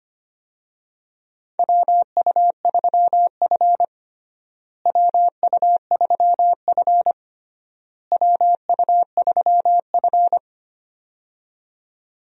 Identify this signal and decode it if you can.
Morse code, heard as a keyed tone, about 25 wpm, 700 Hz. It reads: WU3F WU3F WU3F
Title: USA Call Signs - 4 Characters - Repeated 3x - 25wpm